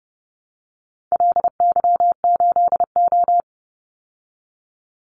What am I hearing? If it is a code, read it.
Morse code, heard as a keyed tone, about 30 wpm, 700 Hz. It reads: LY8O
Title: Call Signs Repeated 2x - 30wpm